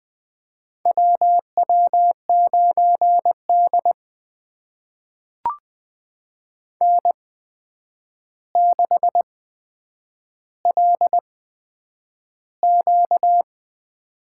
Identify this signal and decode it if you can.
Morse code, heard as a keyed tone, about 20 wpm, spaced as for 9 wpm, 700 Hz. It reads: WW9D N6LQ
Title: USA Call Signs - 4 Characters - 8x Character Spacing - 20wpm